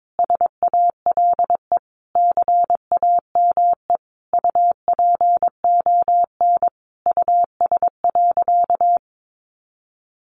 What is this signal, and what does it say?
Morse code, heard as a keyed tone, about 22 wpm, 700 Hz. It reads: SALE CAME UPON US.
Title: Sentences from Top 1400 Words 22wpm